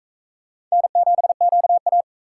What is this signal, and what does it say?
Morse code, heard as a keyed tone, about 38 wpm, 700 Hz. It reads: N7QA